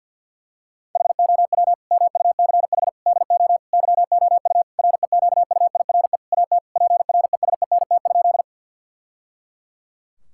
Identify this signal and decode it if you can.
Morse code, heard as a keyed tone, about 50 wpm, 700 Hz. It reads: HOW MUCH DO YOU REQUIRE AT PRESENT?
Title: Sentences from Top 1000 Words - Review Part 6 - 50wpm